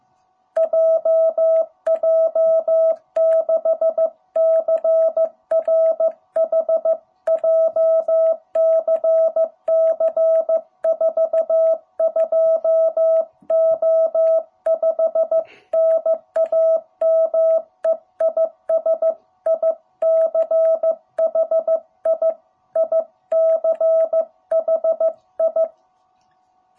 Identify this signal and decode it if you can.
Morse code, heard as a keyed tone, about 15 wpm, 660 Hz. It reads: JJ6CRH JCC 42O5 NAME IS ICHI ICHI
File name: jh8cbh / 2021cw